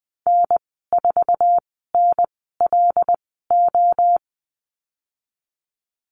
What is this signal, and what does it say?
Morse code, heard as a keyed tone, about 20 wpm, spaced as for 15 wpm, 700 Hz. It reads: N4NLO
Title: 128 Lesson 21g - Number 6 - Call Signs 20wpm